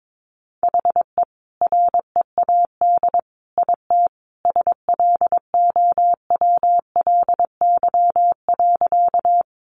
Morse code, heard as a keyed tone, about 22 wpm, 700 Hz. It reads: HE READ IT SLOWLY.
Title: Sentences from Top 1100 Words 22wpm